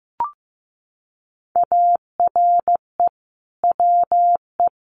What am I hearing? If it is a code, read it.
Morse code, heard as a keyed tone, about 15 wpm, 700 Hz. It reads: ARE WE
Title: Sentences from Top 1500 Words 15wpm